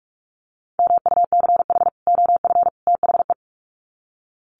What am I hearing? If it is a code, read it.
Morse code, heard as a keyed tone, about 44 wpm, 700 Hz. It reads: MUCH OF THE
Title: Sets of 3 Words 45wpm